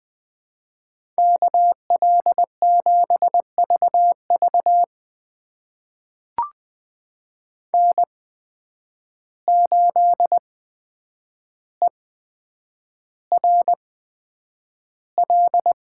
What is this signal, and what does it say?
Morse code, heard as a keyed tone, about 20 wpm, spaced as for 9 wpm, 700 Hz. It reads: KL7VV N8ERL
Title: USA Call Signs - 5 Characters - 8x Character Spacing - 20wpm